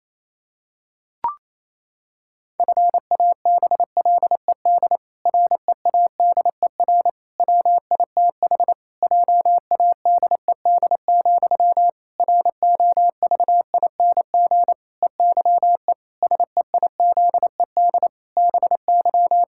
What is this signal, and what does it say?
Morse code, heard as a keyed tone, about 27 wpm, 700 Hz. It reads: FABLED READER WITH JADED, ROVING EYE SEIZED BY
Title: Pangram Challenge v2 - 28wpm